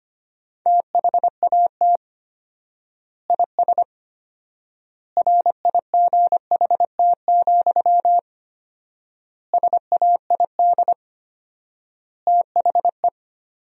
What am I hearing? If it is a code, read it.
Morse code, heard as a keyed tone, about 25 wpm, 700 Hz. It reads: THAT IS RIGHT, SAID THE
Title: 4x Word-Spacing - Sentences from Top 200 Words - 25wpm